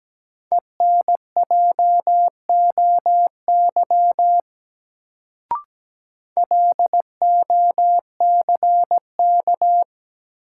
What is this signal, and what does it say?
Morse code, heard as a keyed tone, about 17 wpm, 700 Hz. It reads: ENJOY LOCK
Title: Sentences from Top 1300 Words - New Word Drill Alt - 17wpm